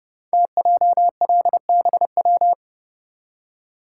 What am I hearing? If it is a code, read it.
Morse code, heard as a keyed tone, about 30 wpm, 700 Hz. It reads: TJLBW
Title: Code Groups - 5 Letter:Numbers - 30wpm